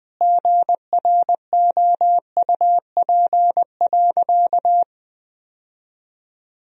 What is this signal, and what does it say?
Morse code, heard as a keyed tone, about 21 wpm, 700 Hz. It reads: GROUP.
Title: Sentences from Top 1000 Words - Review Part 1 - 20wpm